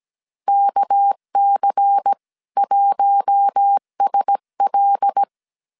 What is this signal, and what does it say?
Morse code, heard as a keyed tone, about 17 wpm, 800 Hz. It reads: KC1SL
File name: 16wpm